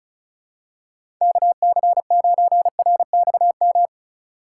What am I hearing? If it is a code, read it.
Morse code, heard as a keyed tone, about 36 wpm, 700 Hz. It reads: KC9RXM
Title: USA Call Signs - 6 Characters - 3x Character Spacing - 35wpm